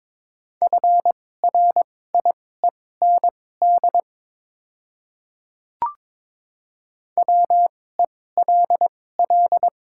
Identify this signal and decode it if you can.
Morse code, heard as a keyed tone, about 22 wpm, spaced as for 16 wpm, 700 Hz. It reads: FRIEND WELL
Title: 2x Character Spacing - Top 300 Words - 22wpm